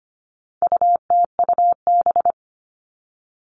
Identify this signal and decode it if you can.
Morse code, heard as a keyed tone, about 25 wpm, 700 Hz. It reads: UTUB
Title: All Cut Numbers - 4-Digit 25wpm